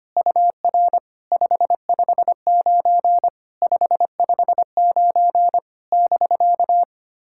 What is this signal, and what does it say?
Morse code, heard as a keyed tone, about 25 wpm, 700 Hz. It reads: UR 559 559 <BK>
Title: SOTA QSO Practice 25wpm